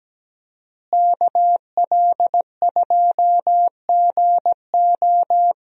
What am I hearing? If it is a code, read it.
Morse code, heard as a keyed tone, about 17 wpm, 700 Hz. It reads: KL2GO
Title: USA Call Signs - 5 Characters - Repeated 2x - 17wpm